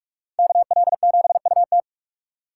Morse code, heard as a keyed tone, about 57 wpm, 700 Hz. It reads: KL7UT